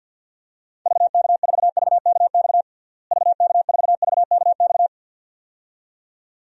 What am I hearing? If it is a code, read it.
Morse code, heard as a keyed tone, about 48 wpm, 700 Hz. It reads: VK4VKX VK4VKX
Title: DXCC-100 Call Signs - 6 Characters - Repeated 2x - 50wpm